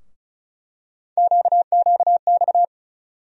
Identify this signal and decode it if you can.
Morse code, heard as a keyed tone, about 36 wpm, 700 Hz. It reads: QQX
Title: USA Call Sign Postfixes - 3 Letters 35wpm